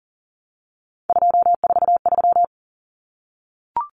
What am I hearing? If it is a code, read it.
Morse code, heard as a keyed tone, about 40 wpm, 700 Hz. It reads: 243
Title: Common Cut Numbers - 3-Digit 40wpm